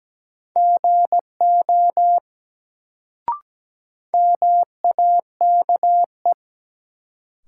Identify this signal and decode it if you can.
Morse code, heard as a keyed tone, about 17 wpm, 700 Hz. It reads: GO MAKE
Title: Sentences from Top 100 Words - New Word Drill - 17wpm